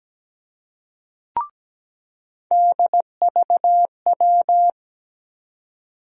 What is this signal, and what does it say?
Morse code, heard as a keyed tone, about 17 wpm, 700 Hz. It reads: DVW
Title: USA Call Sign Postfixes - 3 Letters 17wpm